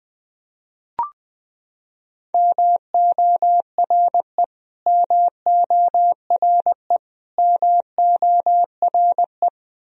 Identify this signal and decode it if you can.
Morse code, heard as a keyed tone, about 20 wpm, 700 Hz. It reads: MORE MORE MORE
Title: Top 100 Words Repeated 20wpm